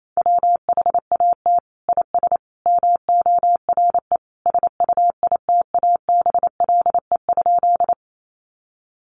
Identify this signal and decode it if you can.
Morse code, heard as a keyed tone, about 28 wpm, 700 Hz. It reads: WHAT IS MORE SUITABLE?
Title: Sentences from Top 1400 Words 28wpm